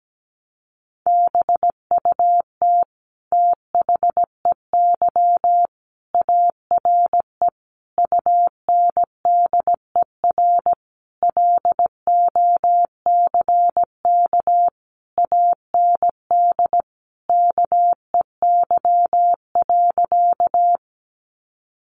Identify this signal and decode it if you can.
Morse code, heard as a keyed tone, about 17 wpm, 700 Hz. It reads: BUT THEY ARE UNDER LOCK AND KEY.
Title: Sentences from Top 1300 Words 17wpm